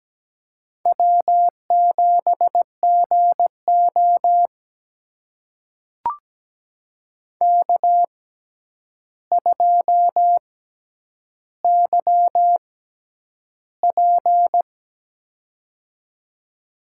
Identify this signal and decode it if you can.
Morse code, heard as a keyed tone, about 17 wpm, spaced as for 9 wpm, 700 Hz. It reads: W7GO K2YP
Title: USA Call Signs - 4 Characters - 6x Character Spacing - 17wpm